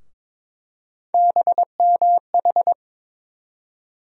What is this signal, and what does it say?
Morse code, heard as a keyed tone, about 22 wpm, 700 Hz. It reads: BMH
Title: USA Call Sign Postfixes - 3 Letters 22wpm